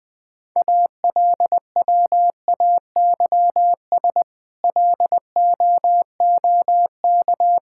Morse code, heard as a keyed tone, about 20 wpm, 700 Hz. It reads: ALWAYS LOOK
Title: Sentences with 8-Word Phrases - 20wpm